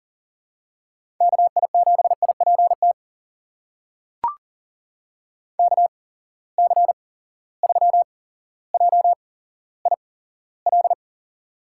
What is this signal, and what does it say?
Morse code, heard as a keyed tone, about 40 wpm, spaced as for 17 wpm, 700 Hz. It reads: KI7IPT KC3JIL
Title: USA Call Signs - 6 Characters - 8x Character Spacing - 40wpm